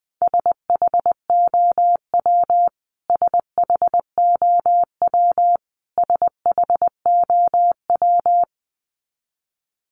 Morse code, heard as a keyed tone, about 20 wpm, 700 Hz. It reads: SHOW SHOW SHOW
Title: Top 500 Words Repeated 20wpm